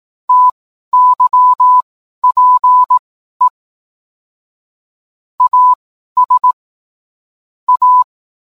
Morse code, heard as a keyed tone, about 18 wpm, spaced as for 12 wpm, 1000 Hz. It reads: TYPE AS A